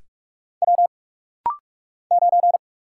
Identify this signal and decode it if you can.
Morse code, heard as a keyed tone, about 46 wpm, 700 Hz. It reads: W 9
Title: Single Letter-Number - Rapid-Fire 45wpm